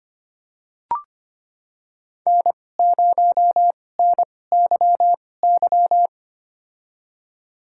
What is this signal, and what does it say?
Morse code, heard as a keyed tone, about 25 wpm, spaced as for 18 wpm, 700 Hz. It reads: N0NYY
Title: USA Call Signs - 5 Characters - 2x Character Spacing - 25wpm